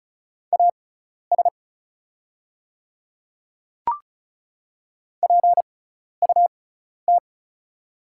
Morse code, heard as a keyed tone, about 36 wpm, spaced as for 12 wpm, 690 Hz. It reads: AS PUT
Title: 6x Character Spacing - Top 200 Words - 35wpm